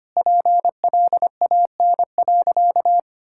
Morse code, heard as a keyed tone, about 25 wpm, 700 Hz. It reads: PLAN.
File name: practice / 25wpm / Sentences with 2-Word Phrases - 25wpm